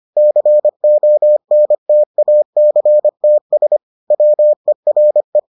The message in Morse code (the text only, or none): CONTACTS WERE